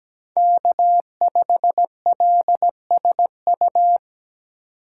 Morse code, heard as a keyed tone, about 20 wpm, 700 Hz. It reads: K5LSU
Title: Speed Racing - USA 1x3 Call Signs 17wpm